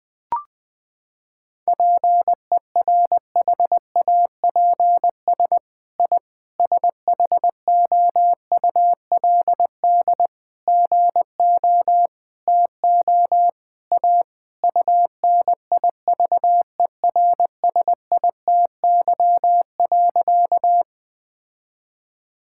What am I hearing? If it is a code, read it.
Morse code, heard as a keyed tone, about 20 wpm, 700 Hz. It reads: PERHAPS I SHOULD GO TO A UNIVERSITY.
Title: Sentences from Top 1000 Words - Review Part 5 - 20wpm